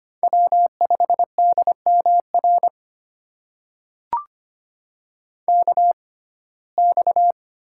Morse code, heard as a keyed tone, about 25 wpm, spaced as for 18 wpm, 700 Hz. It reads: W5DMR KX
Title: USA Call Signs - 5 Characters - 6x Character Spacing - 25wpm